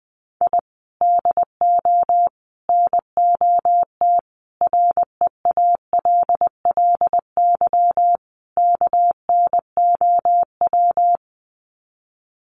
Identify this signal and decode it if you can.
Morse code, heard as a keyed tone, about 20 wpm, 700 Hz. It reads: I DO NOT REALLY KNOW
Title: Sets of 5 Words Encore 20wpm